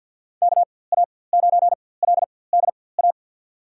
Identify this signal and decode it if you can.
Morse code, heard as a keyed tone, about 49 wpm, spaced as for 23 wpm, 700 Hz. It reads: KA9LDA
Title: USA Call Signs - 6 Characters - 4x Character Spacing - 50wpm